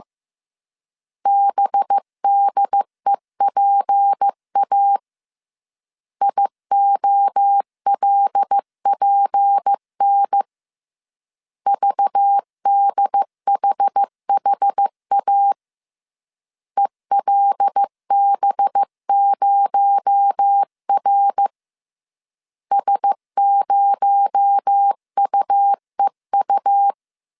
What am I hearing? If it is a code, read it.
Morse code, heard as a keyed tone, about 15 wpm, 800 Hz. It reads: BDEPA IOLPN VDHHA ELB0R S0UEU
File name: CW / DET / 1 6 1500C S C 14wpm 2951